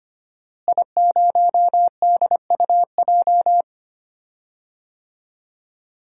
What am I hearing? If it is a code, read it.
Morse code, heard as a keyed tone, about 25 wpm, 700 Hz. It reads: I0DUJ
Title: DXCC-100 Call Signs - 5 Characters - Repeated 2x - 25wpm